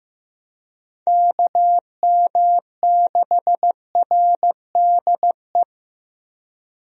Morse code, heard as a keyed tone, about 15 wpm, 700 Hz. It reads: KM6RDE
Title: USA Call Signs - 6 Characters - Repeated 2x - 15wpm